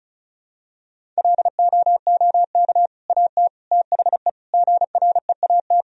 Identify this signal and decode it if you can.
Morse code, heard as a keyed tone, about 36 wpm, 700 Hz. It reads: LOOK AT THE GREAT
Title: Sentences from Top 1000 Words - Review Part 1 - 35wpm